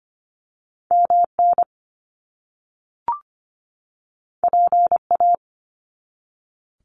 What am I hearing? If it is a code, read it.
Morse code, heard as a keyed tone, about 25 wpm, 700 Hz. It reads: MN PA
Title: US States 25wpm